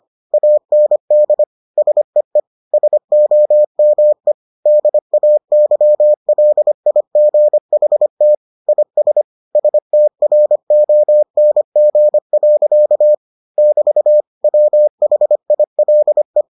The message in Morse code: AND SEE SOME DAYLIGHT IS STRONG. <BT> WHILE

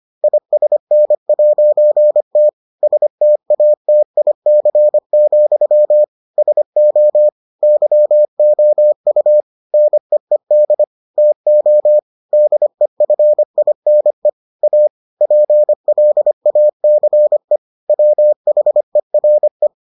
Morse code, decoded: ISN'T STATIC, SO YOU NEED TO DEFINE A PLACE WHERE

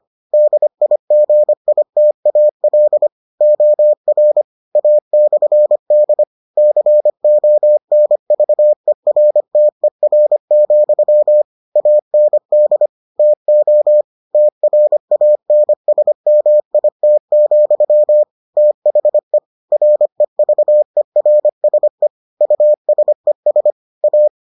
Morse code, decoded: DIGITAL OR A/D CONVERTER, AND TO TRANSMIT, THE REVERSE USES A

